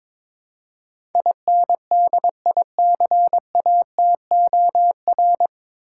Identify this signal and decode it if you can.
Morse code, heard as a keyed tone, about 22 wpm, 700 Hz. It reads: INDICATOR
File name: 9 Letter Words - First 3 Letters Given - 22wpm